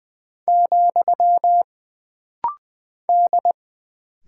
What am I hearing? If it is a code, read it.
Morse code, heard as a keyed tone, about 20 wpm, 700 Hz. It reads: , D
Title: Single Character (Letter:Number:Punctuation:Prosign) - Rapid-Fire - 20wpm